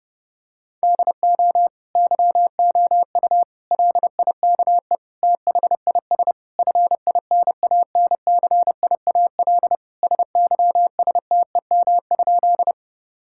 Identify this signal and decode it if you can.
Morse code, heard as a keyed tone, about 32 wpm, 700 Hz. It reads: DO YOU LIKE THIS FINANCIAL SYSTEM?
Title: Sentences from Top 1000 Words - Review Part 5 - 30wpm